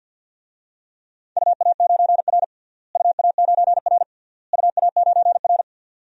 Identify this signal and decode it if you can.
Morse code, heard as a keyed tone, about 53 wpm, 700 Hz. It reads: UA9R UA9R UA9R
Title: DXCC-100 Call Signs - 4 Characters - Repeated 3x - 50wpm